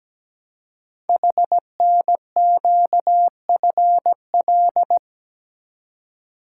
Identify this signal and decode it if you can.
Morse code, heard as a keyed tone, about 17 wpm, 700 Hz. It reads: HNQFL
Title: Code Groups - 5 Letters - 17wpm